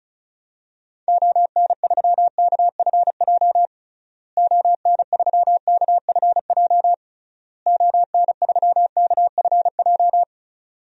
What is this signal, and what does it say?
Morse code, heard as a keyed tone, about 34 wpm, 700 Hz. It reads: ON3KFJ ON3KFJ ON3KFJ
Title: DXCC-100 Call Signs - 6 Characters - Repeated 3x - 35wpm